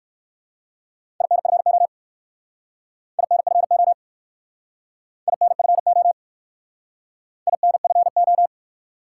Morse code, heard as a keyed tone, about 58 wpm, 700 Hz. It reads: INFO INFO INFO INFO